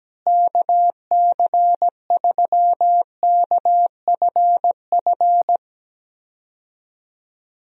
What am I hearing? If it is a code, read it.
Morse code, heard as a keyed tone, about 17 wpm, 700 Hz. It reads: KC3KFF
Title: USA Call Signs - 6 Characters - Repeated 2x - 17wpm